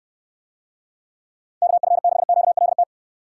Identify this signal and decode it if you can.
Morse code, heard as a keyed tone, about 69 wpm, 700 Hz. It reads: KF6QLT